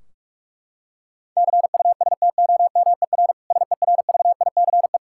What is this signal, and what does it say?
Morse code, heard as a keyed tone, about 41 wpm, 700 Hz. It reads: CUSTOMER SERVICE
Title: Practice QSO - Exchange 3 - More Info - 45wpm